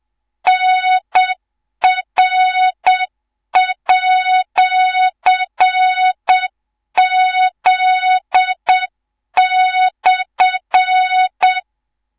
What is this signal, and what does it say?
Morse code, heard as a keyed tone, about 7 wpm, 3700 Hz, with noise. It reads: NR@Z/
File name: Morze-HA2MN / hang / 18 IRASJELEK2